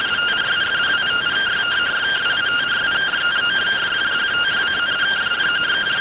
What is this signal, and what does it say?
Two-tone teleprinter signal, about 170 Hz shift.